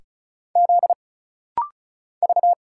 Morse code, heard as a keyed tone, about 34 wpm, 700 Hz. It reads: Z V
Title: Single Letters - Rapid-Fire 35wpm